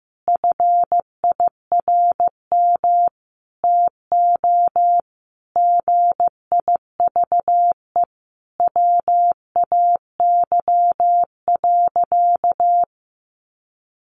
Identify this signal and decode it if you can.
Morse code, heard as a keyed tone, about 15 wpm, 700 Hz. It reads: FIRM TO GIVE WAY.
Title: Sentences from Top 1400 Words 15wpm